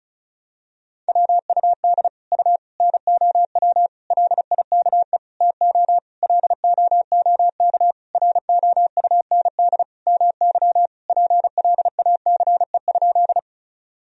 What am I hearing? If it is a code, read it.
Morse code, heard as a keyed tone, about 34 wpm, 700 Hz. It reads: WUD U NOW LIKE TO LOOK ROUND MY PLACE?
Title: Sentences from Top 200 Words and Abbreviations 35wpm